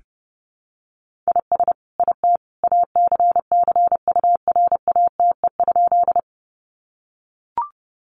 Morse code, heard as a keyed tone, about 30 wpm, 700 Hz. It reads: IS IT ACCURATE?
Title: Sentences from Top 1200 Words 30wpm